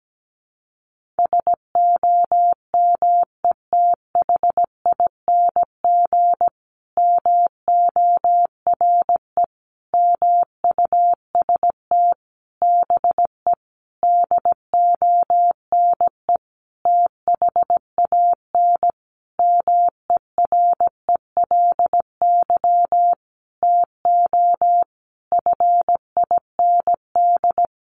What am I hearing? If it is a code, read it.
Morse code, heard as a keyed tone, about 17 wpm, 700 Hz. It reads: SOMETHING MORE MUST BE DONE THAN MERELY TO FIND
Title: Sentences from Top 1400 Words 17wpm